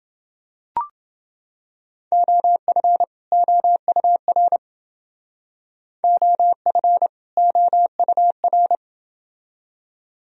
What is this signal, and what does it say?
Morse code, heard as a keyed tone, about 28 wpm, 700 Hz. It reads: OF OUR OF OUR